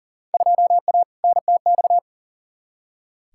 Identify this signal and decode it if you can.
Morse code, heard as a keyed tone, about 40 wpm, 700 Hz. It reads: 2A NTX
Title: Field Day Exchanges - 40wpm